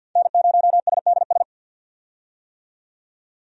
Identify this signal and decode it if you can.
Morse code, heard as a keyed tone, about 53 wpm, 700 Hz. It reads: N0SDS
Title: USA Call Signs - 5 Characters - 50wpm